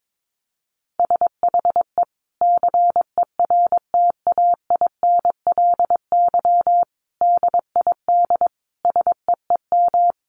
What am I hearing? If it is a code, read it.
Morse code, heard as a keyed tone, about 22 wpm, 700 Hz. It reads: SHE CERTAINLY DID SEEM
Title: Sentences from Top 1100 Words 22wpm